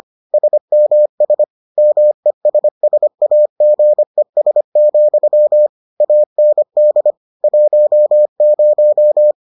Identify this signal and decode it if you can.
Morse code, heard as a keyed tone, about 25 wpm, 600 Hz. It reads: SMS MESSAGES, AND 10